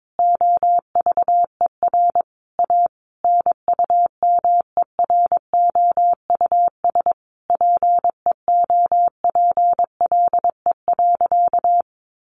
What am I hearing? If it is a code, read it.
Morse code, heard as a keyed tone, about 22 wpm, 700 Hz. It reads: OVER A NUMEROUS PEOPLE.
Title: Sentences from Top 1400 Words 22wpm